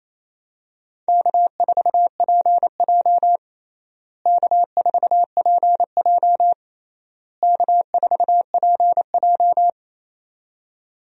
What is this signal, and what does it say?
Morse code, heard as a keyed tone, about 28 wpm, 700 Hz. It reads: K4PJ K4PJ K4PJ
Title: USA Call Signs - 4 Characters - Repeated 3x - 28wpm